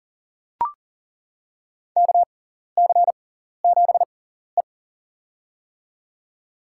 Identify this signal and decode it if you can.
Morse code, heard as a keyed tone, about 37 wpm, spaced as for 14 wpm, 700 Hz. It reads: KC7E
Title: USA Call Signs - 4 Characters - 6x Character Spacing - 40wpm